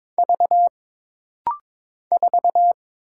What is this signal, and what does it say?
Morse code, heard as a keyed tone, about 22 wpm, 700 Hz. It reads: V 4
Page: This rapid-fire timing is geared to push your ICR (Instant Character Recognition) skills.